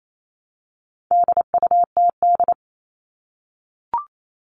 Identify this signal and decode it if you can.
Morse code, heard as a keyed tone, about 28 wpm, 700 Hz. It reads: DUTD
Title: All Cut Numbers - 4-Digit 28wpm